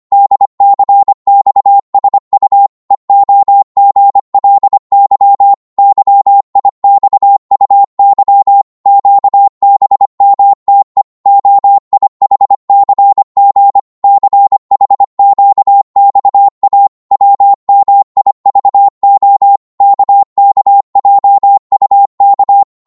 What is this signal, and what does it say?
Morse code, heard as a keyed tone, about 25 wpm, 800 Hz. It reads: DCXSU EOGLY YIXUY QBMTE OIHCG CHQXA WMIVO KKJUK
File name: Morse / ltr-25wpm